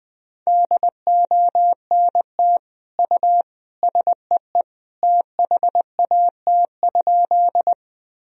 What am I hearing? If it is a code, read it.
Morse code, heard as a keyed tone, about 20 wpm, 700 Hz. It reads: DONT U SEE THAT?
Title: Sentences from Top 300 Words and Abbreviations 20wpm